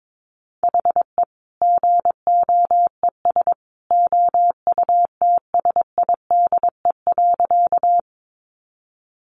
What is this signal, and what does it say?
Morse code, heard as a keyed tone, about 22 wpm, 700 Hz. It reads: HE GOES OUTSIDE.
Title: Additional Easy Phrases 22wpm